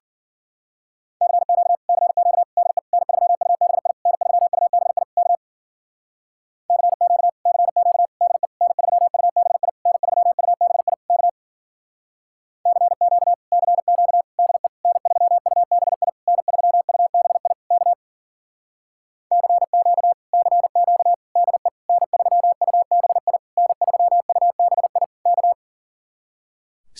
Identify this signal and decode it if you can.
Morse code, heard as a keyed tone, about 52 wpm, 700 Hz. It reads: CQ CQ DE N3UBI N3UBI K CQ CQ DE N3UBI N3UBI K CQ CQ DE N3UBI N3UBI K CQ CQ DE N3UBI N3UBI K